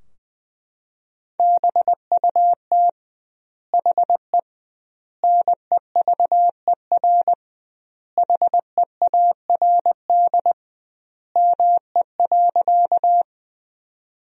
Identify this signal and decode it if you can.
Morse code, heard as a keyed tone, about 20 wpm, 700 Hz. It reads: BUT HE NEVER HEARD ME.
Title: 2x Word-Spacing - Sentences from Top 400 Words - 20wpm